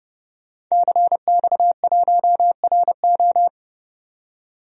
Morse code, heard as a keyed tone, about 30 wpm, 700 Hz. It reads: CX1RO
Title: DXCC-100 Call Signs - 5 Characters - 30wpm